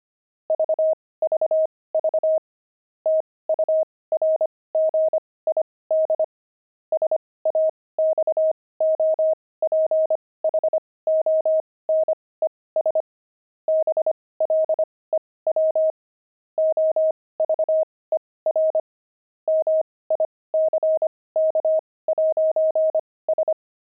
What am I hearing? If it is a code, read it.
Morse code, heard as a keyed tone, about 25 wpm, spaced as for 18 wpm, 630 Hz. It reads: VVV TURGID SAXOPHONES BLEW OVER MICK'S
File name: pangram-fs-18-cs-25-saxophones